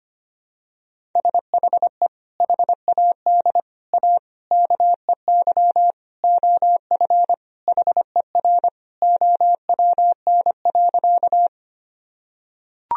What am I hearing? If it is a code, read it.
Morse code, heard as a keyed tone, about 25 wpm, 700 Hz. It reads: SHE HAD A KEY OF HER OWN.
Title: Sentences from Top 1000 Words - Review Part 4 - 25wpm